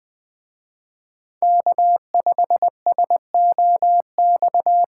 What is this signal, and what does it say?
Morse code, heard as a keyed tone, about 24 wpm, 700 Hz. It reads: K5SOX
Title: Speed Racing - USA 1x3 Call Signs 20wpm